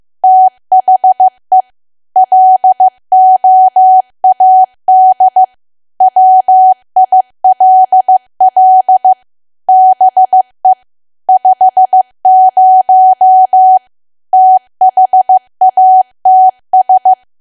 Morse code, heard as a keyed tone, about 15 wpm, 750 Hz. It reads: THE LOAD WILL BE 50 THATS
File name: cwa / W1AW-Level2 / 15WPM